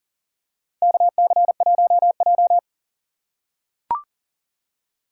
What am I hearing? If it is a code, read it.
Morse code, heard as a keyed tone, about 39 wpm, 700 Hz. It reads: KC1J
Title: USA Call Signs - 4 Characters - 3x Character Spacing - 40wpm